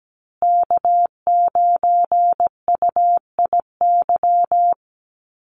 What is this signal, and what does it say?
Morse code, heard as a keyed tone, about 17 wpm, 700 Hz. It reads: K9UIY
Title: Call Signs Repeated 2x - 17wpm